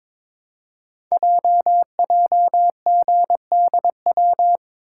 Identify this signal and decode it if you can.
Morse code, heard as a keyed tone, about 22 wpm, 700 Hz. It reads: JJGDW
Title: Code Groups - 5 Letters - 22wpm